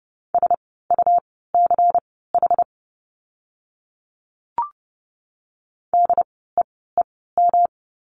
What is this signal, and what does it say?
Morse code, heard as a keyed tone, about 30 wpm, spaced as for 17 wpm, 700 Hz. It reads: SUCH DEEM
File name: course / 30:10wpm / 108 Lesson 18f - Letter m - Words 30wpm